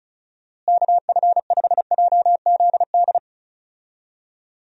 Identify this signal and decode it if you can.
Morse code, heard as a keyed tone, about 34 wpm, 700 Hz. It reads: KF5JZD
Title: USA Call Signs - 6 Characters - Repeated 2x - 35wpm